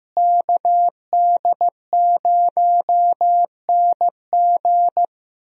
Morse code, heard as a keyed tone, about 16 wpm, 700 Hz. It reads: KD0NG
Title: USA Call Signs - 5 Characters - Repeated 3x - 15wpm